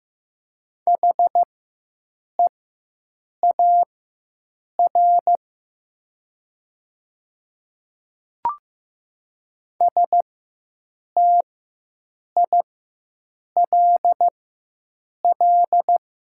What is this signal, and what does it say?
Morse code, heard as a keyed tone, about 15 wpm, spaced as for 7 wpm, 700 Hz. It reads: HEAR STILL
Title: 4x Character Spacing - Top 300 Words - 15wpm